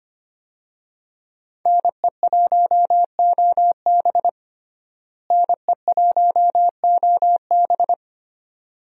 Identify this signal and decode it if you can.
Morse code, heard as a keyed tone, about 25 wpm, 700 Hz. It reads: NE1OB NE1OB
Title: USA Call Signs - 5 Characters - Repeated 2x - 25wpm